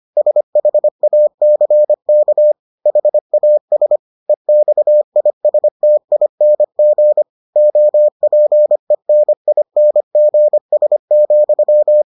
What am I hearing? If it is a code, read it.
Morse code, heard as a keyed tone, about 25 wpm, 600 Hz. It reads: SHACK HAS EXISTING OPENINGS,